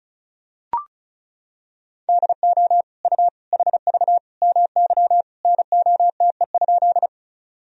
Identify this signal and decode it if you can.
Morse code, heard as a keyed tone, about 36 wpm, 700 Hz. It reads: DO U HV MY NOTE?
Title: Sentences from Top 500 Words and Abbreviations 35wpm